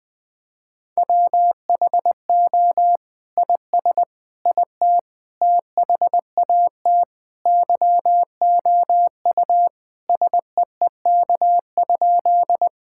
Morse code, heard as a keyed tone, about 21 wpm, 700 Hz. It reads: WHO IS IT THAT YOU SEEK?
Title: Sentences from Top 1600 Words 20wpm